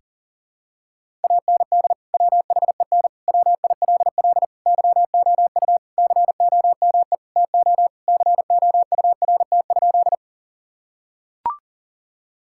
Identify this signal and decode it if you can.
Morse code, heard as a keyed tone, about 40 wpm, 700 Hz. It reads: AND WHEN WILL YOU COME TO COURT?
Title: Sentences from Top 1600 Words 40wpm